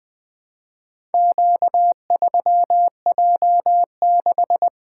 Morse code, heard as a keyed tone, about 20 wpm, 700 Hz. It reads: Q3J6
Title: Four Letter-Number Combinations - 20wpm